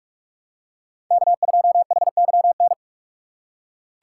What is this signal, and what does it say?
Morse code, heard as a keyed tone, about 42 wpm, 700 Hz. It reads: K2HYN
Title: USA Call Signs - 5 Characters - 4x Character Spacing - 45wpm